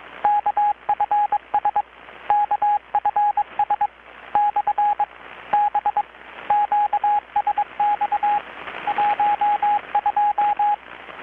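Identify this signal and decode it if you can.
Morse code, heard as a keyed tone, about 22 wpm, 830 Hz, with radio noise. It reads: KFS KFS / B QSX 12
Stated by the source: Morse from KPH's Night of Nights XX maritime radiotelegraphy event, received in Chiapas, Mexico about 3,800 km away on a Kenwood R-600 with a grounded 10 m long-wire antenna.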